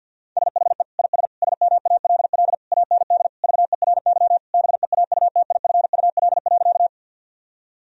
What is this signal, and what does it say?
Morse code, heard as a keyed tone, about 50 wpm, 700 Hz. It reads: SHE IS SMALL AND VERY BEAUTIFUL.
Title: Sentences from Top 1000 Words - Review Part 1 - 50wpm